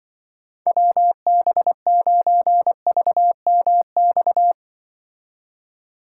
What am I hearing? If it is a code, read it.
Morse code, heard as a keyed tone, about 24 wpm, 700 Hz. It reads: WB9VMX
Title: Speed Racing - USA 2x3 Call Signs 20wpm